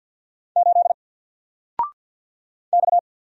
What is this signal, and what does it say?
Morse code, heard as a keyed tone, about 50 wpm, 700 Hz. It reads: 8 X
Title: Single Letter-Number 50wpm